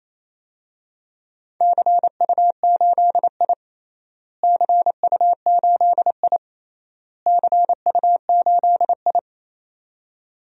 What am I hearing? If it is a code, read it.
Morse code, heard as a keyed tone, about 28 wpm, 700 Hz. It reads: CU8I CU8I CU8I
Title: DXCC-100 Call Signs - 4 Characters - Repeated 3x - 28wpm